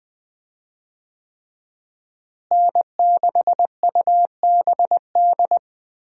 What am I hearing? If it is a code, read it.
Morse code, heard as a keyed tone, about 21 wpm, 700 Hz. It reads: N6UBD
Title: USA Call Signs - 5 Characters - Repeated 3x - 20wpm